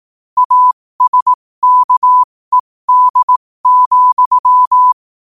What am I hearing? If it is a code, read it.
Morse code, heard as a keyed tone, about 18 wpm, spaced as for 15 wpm, 1000 Hz. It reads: ASKED,
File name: qotdfiles / 15wpm